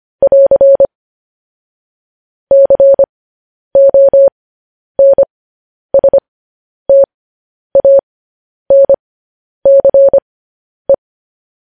Morse code, heard as a keyed tone, about 25 wpm, spaced as for 10 wpm, 550 Hz. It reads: <AR> CONSTANCE
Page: News Headlines in Morse Code at 10 WPM